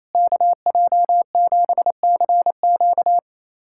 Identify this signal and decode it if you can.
Morse code, heard as a keyed tone, about 27 wpm, 700 Hz. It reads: KJ7CQ
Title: USA Callsign DE Callsign - 5 Characters - 28wpm